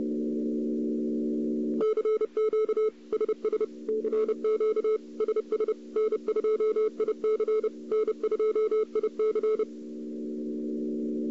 Morse code, heard as a keyed tone, about 30 wpm, 440 Hz. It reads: CQ SS CQ SS N2IC N2IC
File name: CW / 2008-ss-cw-k5tr-op-n5rz -02:00